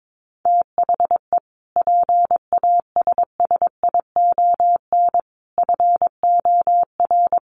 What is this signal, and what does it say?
Morse code, heard as a keyed tone, about 22 wpm, 700 Hz. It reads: THE PASSION FOR
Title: Sentences from Top 1300 Words 22wpm